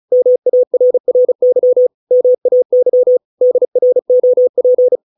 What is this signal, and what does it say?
Morse code, heard as a keyed tone, about 34 wpm, 500 Hz. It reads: MARRY MAY DROP